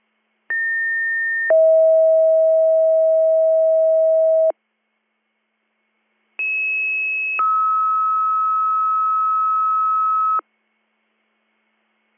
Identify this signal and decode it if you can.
Two-tone page, heard as a one-second tone, then a three-second tone, about 1800 then 640 Hz, about 2600 then 1290 Hz. It is heard from 0.5 to 10.4 s.